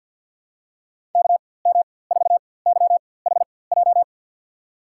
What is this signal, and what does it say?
Morse code, heard as a keyed tone, about 49 wpm, spaced as for 23 wpm, 700 Hz. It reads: KM4YHJ
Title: USA Call Signs - 6 Characters - 4x Character Spacing - 50wpm